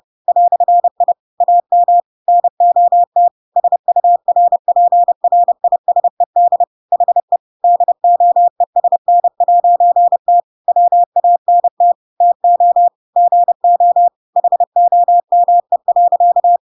Morse code, heard as a keyed tone, about 30 wpm, 700 Hz. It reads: "I AM NOT SURPRISED HE DOESN'T WANT TO GO HOME.
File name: Pinocchio / Pinocchio ch3 30 30 ik1hge